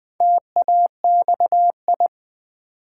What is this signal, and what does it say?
Morse code, heard as a keyed tone, about 20 wpm, 700 Hz. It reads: TAXI
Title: Build A Word - 2 Letters At A Time - 4-Letter Words - 20wpm